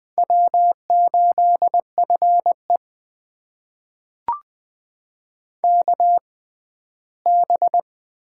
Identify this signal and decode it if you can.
Morse code, heard as a keyed tone, about 20 wpm, spaced as for 14 wpm, 700 Hz. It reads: W8FE KB
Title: USA Call Signs - 6x Character Spacing - 20wpm